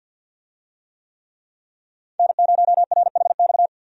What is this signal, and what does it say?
Morse code, heard as a keyed tone, about 46 wpm, 700 Hz. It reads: N0RHX
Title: USA Call Signs - 5 Characters - 50wpm